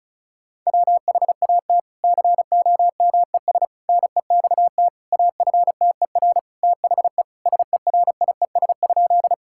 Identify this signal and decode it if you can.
Morse code, heard as a keyed tone, about 36 wpm, 700 Hz. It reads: WHAT COMES NEXT AFTER THE SERIES?
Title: Sentences from Top 1000 Words - Review Part 5 - 35wpm